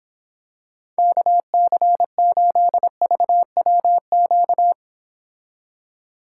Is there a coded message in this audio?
Yes: Morse code from 1.0 to 4.7 s, heard as a keyed tone, about 26 wpm, 700 Hz. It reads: KC8VWQ